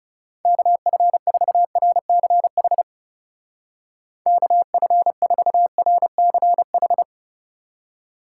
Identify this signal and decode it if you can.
Morse code, heard as a keyed tone, about 39 wpm, 700 Hz. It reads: KF4RCH KF4RCH